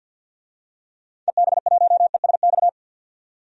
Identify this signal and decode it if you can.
Morse code, heard as a keyed tone, about 52 wpm, 700 Hz. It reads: EB1ESX
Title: DXCC-100 Call Signs - 6 Characters - 50wpm